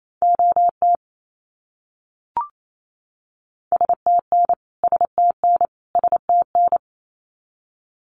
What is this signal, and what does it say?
Morse code, heard as a keyed tone, about 28 wpm, 700 Hz. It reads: OT STN STN STN
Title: Abbreviations Repeated 3x - 28wpm